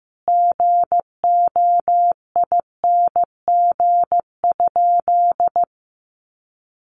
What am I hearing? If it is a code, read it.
Morse code, heard as a keyed tone, about 15 wpm, 700 Hz. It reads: GOING?
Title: Sentences from Top 1300 Words 15wpm